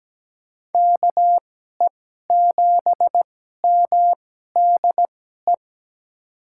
Morse code, heard as a keyed tone, about 17 wpm, spaced as for 12 wpm, 700 Hz. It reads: KE7MDE
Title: USA Call Signs - 6 Characters - 2x Character Spacing - 17wpm